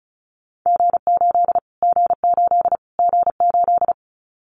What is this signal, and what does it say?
Morse code, heard as a keyed tone, about 34 wpm, 700 Hz. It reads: G8 G8 G8
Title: Callsign Prefixes for Top 25 DXCC - Repeated 3x - 35wpm